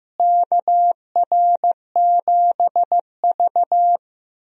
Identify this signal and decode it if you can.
Morse code, heard as a keyed tone, about 16 wpm, 700 Hz. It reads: KR7V
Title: USA Call Signs - 4 Characters - Repeated 2x - 15wpm